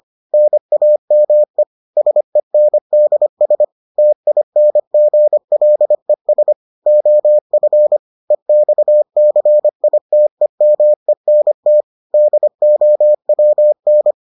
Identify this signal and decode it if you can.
Morse code, heard as a keyed tone, about 25 wpm, 600 Hz. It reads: NAME SENDS TINGLES OF EXCITEMENT DOWN